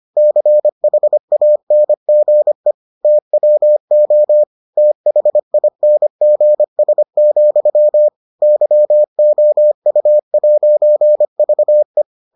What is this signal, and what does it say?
Morse code, heard as a keyed tone, about 25 wpm, 600 Hz. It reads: CHANGE TWO THINGS, YOU'VE